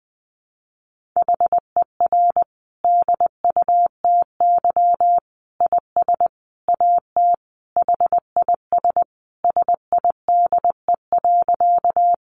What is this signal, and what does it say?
Morse code, heard as a keyed tone, about 20 wpm, 700 Hz. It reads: HER DUTY IS AT HIS SIDE.
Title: Sentences from Top 1500 Words 20wpm